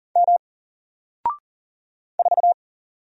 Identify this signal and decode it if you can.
Morse code, heard as a keyed tone, about 40 wpm, 700 Hz. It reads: M 4
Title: Single Letter-Number Alt 40wpm